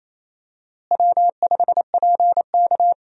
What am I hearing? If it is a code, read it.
Morse code, heard as a keyed tone, about 28 wpm, 700 Hz. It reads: W5PK
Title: USA Call Signs - 4 Characters - Repeated 2x - 28wpm